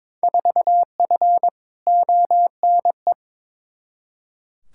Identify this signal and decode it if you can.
Morse code, heard as a keyed tone, about 22 wpm, 700 Hz. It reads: 4F ONE
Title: Field Day Exchanges Encore - 22wpm